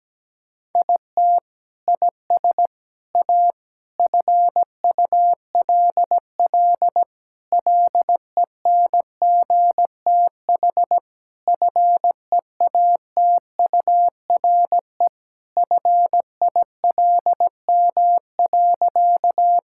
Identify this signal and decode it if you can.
Morse code, heard as a keyed tone, about 17 wpm, 700 Hz. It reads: IT IS A FULL LENGTH FEATURE FILM.
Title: Sentences from Top 1000 Words - Review Part 5 - 17wpm